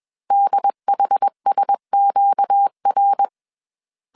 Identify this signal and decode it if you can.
Morse code, heard as a keyed tone, about 21 wpm, 800 Hz. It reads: DHSQR
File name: CW / DET / 1 6 1500C S B 20wpm 2048